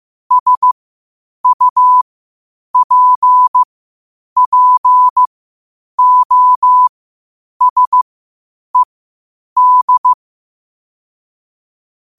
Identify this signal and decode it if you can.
Morse code, heard as a keyed tone, about 15 wpm, spaced as for 8 wpm, 1000 Hz. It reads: SUPPOSED
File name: 8wpm